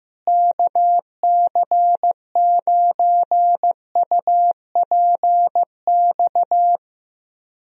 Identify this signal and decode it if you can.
Morse code, heard as a keyed tone, about 15 wpm, 700 Hz. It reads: KC9UPX
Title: USA Call Signs - 6 Characters - Repeated 2x - 15wpm